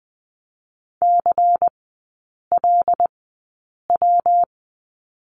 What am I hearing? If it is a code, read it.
Morse code, heard as a keyed tone, about 19 wpm, 700 Hz. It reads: C L W
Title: 205 Lesson 33e - Letter j - Three Characters 20wpm